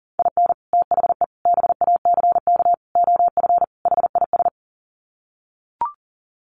Morse code, heard as a keyed tone, about 40 wpm, 700 Hz. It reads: IN THE BACK OF HIS
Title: Sets of 5 Words Encore2 40wpm